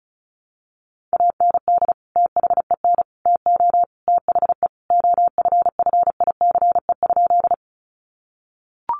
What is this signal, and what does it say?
Morse code, heard as a keyed tone, about 34 wpm, 700 Hz. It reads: AND THEN TO THE OFFICE?
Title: Sentences from Top 1000 Words 35wpm